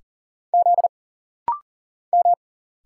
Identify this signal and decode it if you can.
Morse code, heard as a keyed tone, about 40 wpm, 700 Hz. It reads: Z M
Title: Single Letters - Rapid-Fire 40wpm